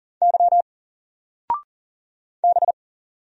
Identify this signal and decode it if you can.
Morse code, heard as a keyed tone, about 40 wpm, 700 Hz. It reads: Y B
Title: Single Character (Letter:Number:Punctuation:Prosign) 40wpm